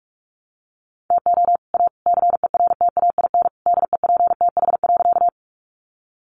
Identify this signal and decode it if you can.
Morse code, heard as a keyed tone, about 44 wpm, 700 Hz. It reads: TO A CERTAIN DEPTH.
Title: Sentences from Top 1200 Words 45wpm